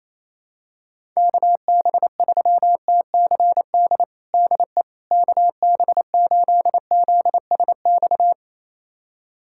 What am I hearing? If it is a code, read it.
Morse code, heard as a keyed tone, about 28 wpm, 700 Hz. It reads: KB3TCD DE KB8ZSX
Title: USA Callsign DE Callsign - 6 Characters - 28wpm